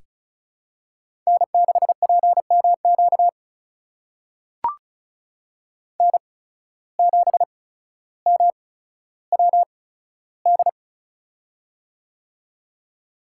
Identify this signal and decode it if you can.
Morse code, heard as a keyed tone, about 34 wpm, spaced as for 15 wpm, 700 Hz. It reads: N6PMQ N7MWD
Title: USA Call Signs - 5 Characters - 8x Character Spacing - 35wpm